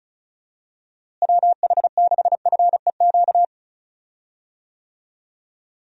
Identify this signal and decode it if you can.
Morse code, heard as a keyed tone, about 34 wpm, 700 Hz. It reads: WH6FEQ
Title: USA Call Signs - 6 Characters - Repeated 2x - 35wpm